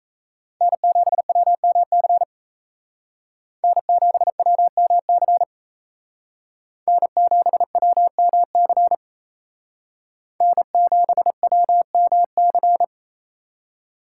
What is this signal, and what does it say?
Morse code, heard as a keyed tone, about 35 wpm, 700 Hz. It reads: N7WMC N7WMC N7WMC N7WMC
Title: Speed Racing - USA 1x3 Call Signs 28wpm